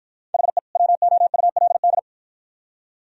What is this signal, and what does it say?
Morse code, heard as a keyed tone, about 49 wpm, 700 Hz. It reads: HE WOULD